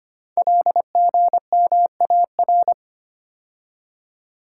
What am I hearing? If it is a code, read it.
Morse code, heard as a keyed tone, about 25 wpm, 700 Hz. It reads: LGMAR
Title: Code Groups - 5 Letters - 25wpm